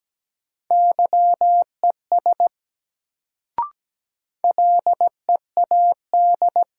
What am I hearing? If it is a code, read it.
Morse code, heard as a keyed tone, about 17 wpm, 700 Hz. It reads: YES LEAD
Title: Sentences from Top 500 Words - New Word Drill - 17wpm